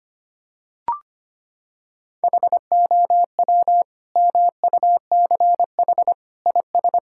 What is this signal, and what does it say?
Morse code, heard as a keyed tone, about 25 wpm, 700 Hz. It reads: HOW MUCH IS
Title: Sentences from Top 1000 Words - Review Part 5 - 25wpm